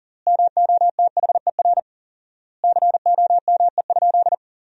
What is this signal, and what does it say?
Morse code, heard as a keyed tone, about 40 wpm, 700 Hz. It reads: MOTHER COME?
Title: 4x Word-Spacing - Sentences from Top 200 Words - 40wpm